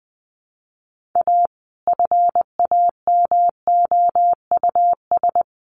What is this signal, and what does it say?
Morse code, heard as a keyed tone, about 20 wpm, 700 Hz. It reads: A FAMOUS
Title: Sentences from Top 1500 Words 20wpm